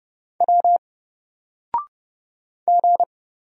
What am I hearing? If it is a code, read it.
Morse code, heard as a keyed tone, about 30 wpm, 700 Hz. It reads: W G
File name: Single-Letter-Number-30wpm